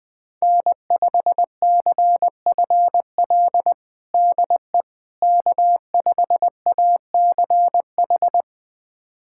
Morse code, heard as a keyed tone, about 20 wpm, 700 Hz. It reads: N5CFL DE K5ACH
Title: USA Callsign DE Callsign - 5 Characters - 20wpm